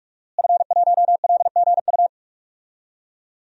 Morse code, heard as a keyed tone, about 46 wpm, 700 Hz. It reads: F1LGU
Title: DXCC-100 Call Signs - 5 Characters - Repeated 2x - 45wpm